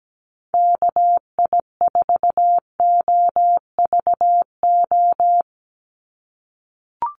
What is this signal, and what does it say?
Morse code, heard as a keyed tone, about 17 wpm, 700 Hz. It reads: KI4OVO
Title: USA Call Signs 17wpm